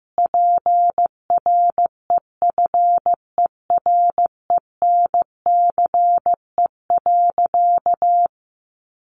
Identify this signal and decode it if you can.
Morse code, heard as a keyed tone, about 15 wpm, 700 Hz. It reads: PREFERENCE.
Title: Sentences from Top 1500 Words 15wpm